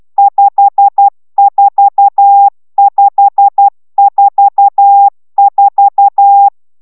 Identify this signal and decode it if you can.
Morse code, heard as a keyed tone, about 12 wpm, 800 Hz. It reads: 54544